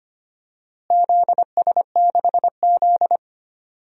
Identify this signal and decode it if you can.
Morse code, heard as a keyed tone, about 25 wpm, 700 Hz. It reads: ZS6Z
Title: DXCC-100 Call Signs - 4 Characters - Repeated 2x - 25wpm